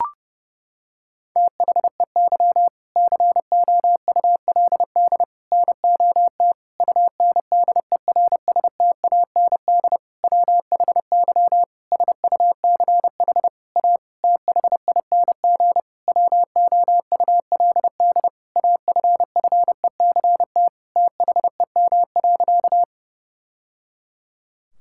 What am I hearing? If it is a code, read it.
Morse code, heard as a keyed tone, about 32 wpm, 700 Hz. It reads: THEY COULD NOT UNDERSTAND WHY SUCH A THING WOULD AFFECT THEM.
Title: Sentences from Top 1000 Words - Review Part 6 - 30wpm